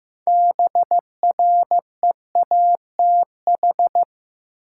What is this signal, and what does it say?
Morse code, heard as a keyed tone, about 15 wpm, 700 Hz. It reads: BREATH
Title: Six Letter Words - 15wpm